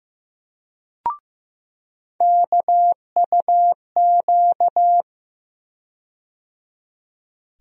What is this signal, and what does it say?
Morse code, heard as a keyed tone, about 15 wpm, 700 Hz. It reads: KUQ
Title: USA Call Sign Postfixes - 3 Letters 15wpm